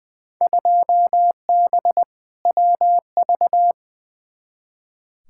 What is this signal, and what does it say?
Morse code, heard as a keyed tone, about 20 wpm, 700 Hz. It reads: 2B WV
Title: Field Day Exchanges Encore - 20wpm